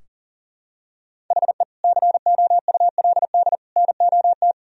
Morse code, heard as a keyed tone, about 37 wpm, 700 Hz. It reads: HE COULD NOT
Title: Sentences from Top 1000 Words - Review Part 2 - 40wpm